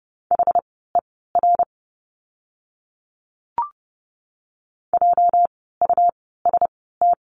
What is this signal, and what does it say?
Morse code, heard as a keyed tone, about 30 wpm, spaced as for 17 wpm, 700 Hz. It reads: HER JUST
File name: course / 30:10wpm / 230 Lesson 37f - Letter x - Words 30wpm